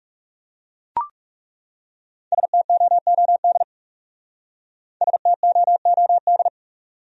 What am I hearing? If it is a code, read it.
Morse code, heard as a keyed tone, about 37 wpm, 700 Hz. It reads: STOOD STOOD
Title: Speed-Racing Top 500 Words 30wpm